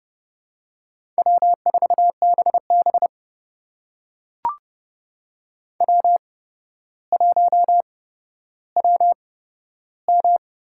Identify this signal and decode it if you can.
Morse code, heard as a keyed tone, about 29 wpm, spaced as for 13 wpm, 700 Hz. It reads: W4BB W1WM
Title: USA Call Signs - 8x Character Spacing - 30wpm